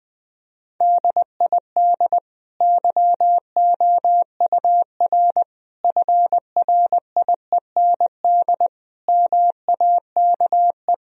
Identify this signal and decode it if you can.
Morse code, heard as a keyed tone, about 19 wpm, 700 Hz. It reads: DID YOUR FRIEND MAKE
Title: Sentences from Top 1000 Words - Review Part 3 - 20wpm